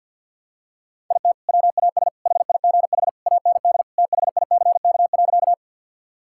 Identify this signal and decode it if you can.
Morse code, heard as a keyed tone, about 50 wpm, 700 Hz. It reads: IT WAS HIGH AND THICK.
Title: Sentences from Top 1600 Words 50wpm